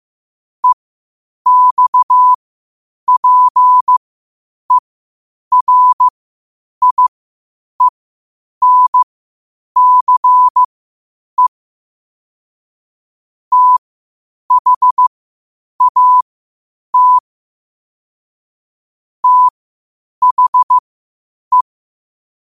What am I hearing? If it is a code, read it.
Morse code, heard as a keyed tone, about 15 wpm, spaced as for 8 wpm, 1000 Hz. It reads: EXPERIENCE THAT THE
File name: qotdfiles / 8wpm